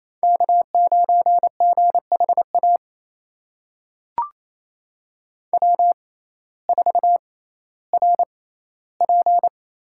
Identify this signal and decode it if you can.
Morse code, heard as a keyed tone, about 27 wpm, spaced as for 15 wpm, 700 Hz. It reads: K9GHA W4RP
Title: USA Call Signs - 5 Characters - 6x Character Spacing - 28wpm